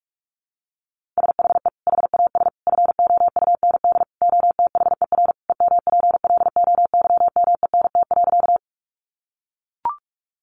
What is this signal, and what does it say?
Morse code, heard as a keyed tone, about 44 wpm, 700 Hz. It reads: SHE HAS FOUND OTHER EMPLOYMENT.
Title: Sentences from Top 1500 Words 45wpm